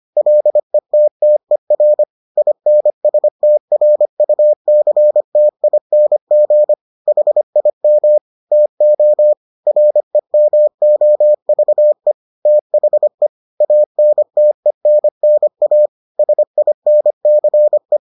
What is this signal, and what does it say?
Morse code, heard as a keyed tone, about 25 wpm, 600 Hz. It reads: LETTER INSTRUCTING HIM TO REMOVE THE ANTENNA SINCE